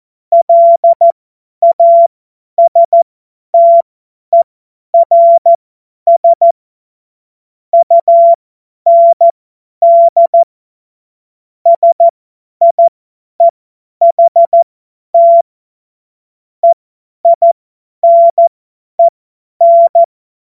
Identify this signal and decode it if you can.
Morse code, heard as a keyed tone, about 14 wpm, spaced as for 10 wpm, 680 Hz. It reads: LASTERS UND SIEHT EINEN
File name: Morsekurs / Witze / Witz 07.November 2023 50 F70